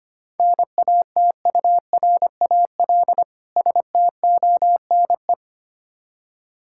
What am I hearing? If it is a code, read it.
Morse code, heard as a keyed tone, about 25 wpm, 690 Hz. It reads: NATURAL STONE
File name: practice / 25wpm / Difficult Words - Speed-Racing with Hints - 25wpm